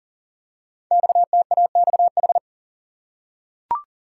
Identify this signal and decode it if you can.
Morse code, heard as a keyed tone, about 38 wpm, 690 Hz. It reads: XTAXH
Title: Code Groups - 5 Letters - 40wpm